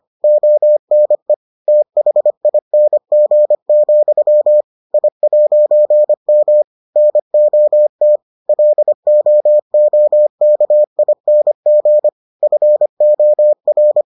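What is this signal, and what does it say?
Morse code, heard as a keyed tone, about 25 wpm, 600 Hz. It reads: ONE THING, I'M NOT LOOKING FOR
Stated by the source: Morse edition of the Foundations of Amateur Radio episodes.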